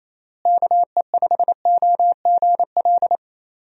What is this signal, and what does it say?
Morse code, heard as a keyed tone, about 28 wpm, 700 Hz. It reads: KE5OGL
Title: USA Call Signs - 6 Characters - Repeated 2x - 28wpm